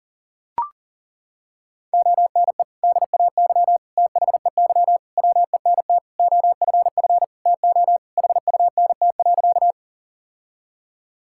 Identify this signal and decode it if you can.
Morse code, heard as a keyed tone, about 40 wpm, 700 Hz. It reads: ONE DAY THEY WENT OFF TO HUNT.
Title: Sentences from Top 1600 Words 40wpm